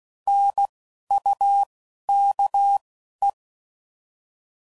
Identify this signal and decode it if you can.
Morse code, heard as a keyed tone, about 16 wpm, spaced as for 11 wpm, 800 Hz. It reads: NUKE
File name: Prefixes / By First Letter / Ns